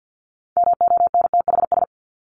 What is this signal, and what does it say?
Morse code, heard as a keyed tone, about 50 wpm, 700 Hz. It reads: MONTHS
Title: ES:S Suffix - Top 100 Words with Base Words 50wpm